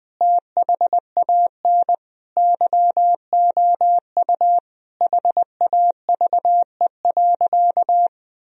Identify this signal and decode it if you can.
Morse code, heard as a keyed tone, about 20 wpm, 700 Hz. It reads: THAN YOU HAVE.
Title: Sentences from Top 1000 Words - Review Part 3 - 20wpm